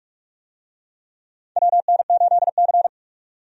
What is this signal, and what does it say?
Morse code, heard as a keyed tone, about 41 wpm, 700 Hz. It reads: WN8C
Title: USA Call Signs - 4 Characters - 45wpm